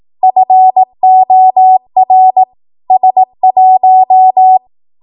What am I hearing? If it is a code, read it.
Morse code, heard as a keyed tone, about 18 wpm, 750 Hz. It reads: FOR S1
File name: Morse / 18WPM